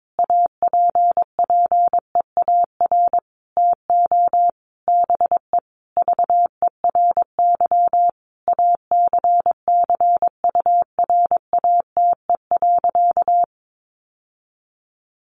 Morse code, heard as a keyed tone, about 22 wpm, 700 Hz. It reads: APPEAR TO BE VERY ACCURATE.
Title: Sentences from Top 1200 Words 22wpm